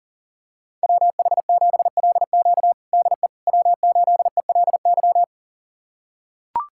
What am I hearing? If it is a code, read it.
Morse code, heard as a keyed tone, about 40 wpm, 700 Hz. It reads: WH7LQ DE W8ELY
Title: USA Callsign DE Callsign - 5 Characters - 40wpm